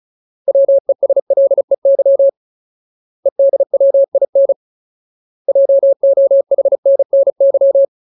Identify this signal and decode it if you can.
Morse code, heard as a keyed tone, about 34 wpm, 550 Hz. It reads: WESLEY EDWIN JOHNNY
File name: Ultimate Word List / 35WPM Ultimate